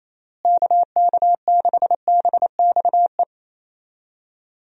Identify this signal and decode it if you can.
Morse code, heard as a keyed tone, about 28 wpm, 700 Hz. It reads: KK6BXE
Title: USA Call Signs - 6 Characters - Repeated 3x - 28wpm